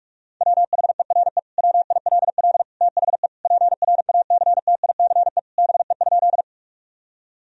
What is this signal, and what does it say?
Morse code, heard as a keyed tone, about 42 wpm, 700 Hz. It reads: WHERE WILL THE PRACTICE BE?
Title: Sentences from Top 1000 Words - Review Part 4 - 45wpm